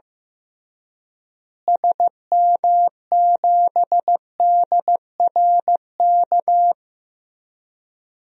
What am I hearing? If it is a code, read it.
Morse code, heard as a keyed tone, about 15 wpm, 700 Hz. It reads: SM7DRK
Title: DXCC-100 Call Signs - 6 Characters - Repeated 2x - 15wpm